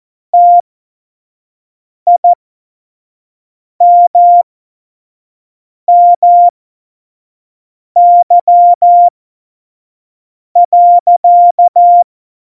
Morse code, heard as a keyed tone, about 14 wpm, spaced as for 5 wpm, 700 Hz. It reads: TIMMY.
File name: Morsekurs / Witze / Witz 07. November 2017, 25 F70